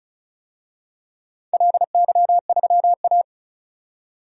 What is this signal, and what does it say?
Morse code, heard as a keyed tone, about 36 wpm, 700 Hz. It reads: LY3A
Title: DXCC-100 Call Signs - 4 Characters - 35wpm